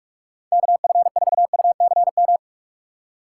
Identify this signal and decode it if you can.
Morse code, heard as a keyed tone, about 47 wpm, 700 Hz. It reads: KF4UCM
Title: USA Call Signs - 6 Characters - 6x Character Spacing - 45wpm